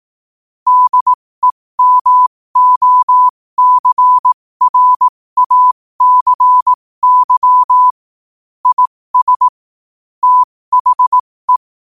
Morse code, heard as a keyed tone, about 18 wpm, spaced as for 15 wpm, 1000 Hz. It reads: DEMOCRACY IS THE
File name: qotdfiles / 15wpm